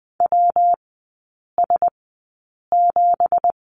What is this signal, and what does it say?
Morse code, heard as a keyed tone, about 20 wpm, spaced as for 8 wpm, 690 Hz. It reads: WS7
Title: 166 Lesson 27e - Number 7 - Three Characters 20wpm